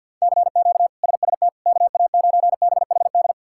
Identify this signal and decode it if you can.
Morse code, heard as a keyed tone, about 47 wpm, 690 Hz. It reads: CQ SST KA9BHD
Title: K1USN SST QSO Practice - 50wpm